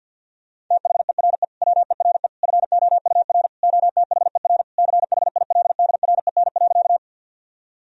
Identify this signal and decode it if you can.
Morse code, heard as a keyed tone, about 52 wpm, 700 Hz. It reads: THERE WERE FOUR OTHER CHILDREN.
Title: Sentences from Top 1000 Words - Review Part 1 - 50wpm